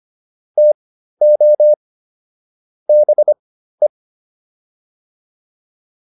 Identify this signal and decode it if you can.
Morse code, heard as a keyed tone, about 25 wpm, spaced as for 13 wpm, 600 Hz. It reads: TO BE